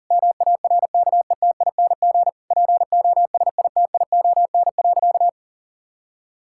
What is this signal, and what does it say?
Morse code, heard as a keyed tone, about 40 wpm, 700 Hz. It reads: MARKETING POSITION.
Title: Sentences from Top 1000 Words - Review Part 5 - 40wpm